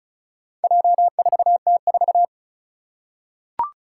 Morse code, heard as a keyed tone, about 36 wpm, 700 Hz. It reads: J4T4
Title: Four Challenging Letter-Number Combinations 35wpm